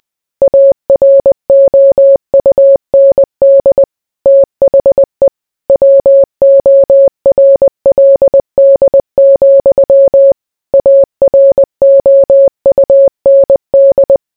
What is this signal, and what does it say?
Morse code, heard as a keyed tone, about 20 wpm, 550 Hz. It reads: AROUND THE WORLD, AROUND